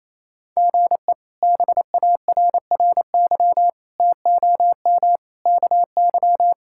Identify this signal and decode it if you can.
Morse code, heard as a keyed tone, about 28 wpm, 700 Hz. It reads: GE BARRY TOM KY
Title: K1USN SST QSO Practice - 28wpm